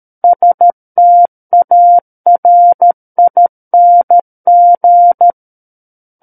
Morse code, heard as a keyed tone, about 13 wpm, 700 Hz. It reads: STARING
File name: Book One / 04-Chapter 4 - 13 WPM